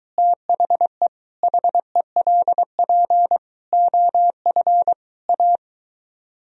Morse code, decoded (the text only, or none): THE HELP OF A